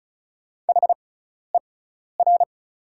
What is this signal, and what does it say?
Morse code, heard as a keyed tone, about 34 wpm, spaced as for 12 wpm, 700 Hz. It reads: HER
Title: 6x Character Spacing - Top 500 Words - 35wpm